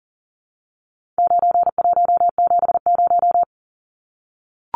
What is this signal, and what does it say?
Morse code, heard as a keyed tone, about 40 wpm, 700 Hz. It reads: 9170
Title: Numbers - 4-Digit 40wpm